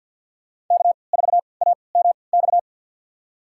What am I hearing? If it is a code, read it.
Morse code, heard as a keyed tone, about 51 wpm, spaced as for 28 wpm, 700 Hz. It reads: K4AMX
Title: USA Call Signs - 5 Characters - 3x Character Spacing - 50wpm